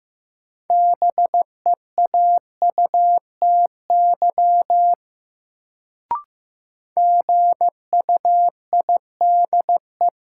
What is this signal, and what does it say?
Morse code, heard as a keyed tone, about 15 wpm, 700 Hz. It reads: BEAUTY GUIDE
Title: Sentences from Top 600 Words - New Word Drill - 15wpm